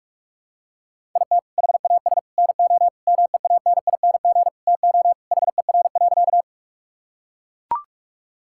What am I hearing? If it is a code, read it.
Morse code, heard as a keyed tone, about 41 wpm, 700 Hz. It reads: IT HAS NO MEANING TO HER.
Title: Sentences from Top 1000 Words - Review Part 6 - 45wpm